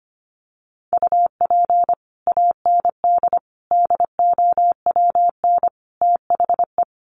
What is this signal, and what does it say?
Morse code, heard as a keyed tone, about 25 wpm, 700 Hz. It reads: UP AND DOWN THE
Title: Sets of 4 Words 25wpm